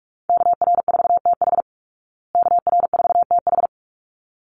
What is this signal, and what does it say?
Morse code, heard as a keyed tone, about 44 wpm, 700 Hz. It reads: KR4TH KR4TH
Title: Call Signs Repeated 3x - 45wpm